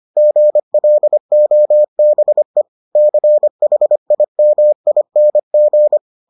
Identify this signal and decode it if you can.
Morse code, heard as a keyed tone, about 25 wpm, 600 Hz. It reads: GLOBE CHIMING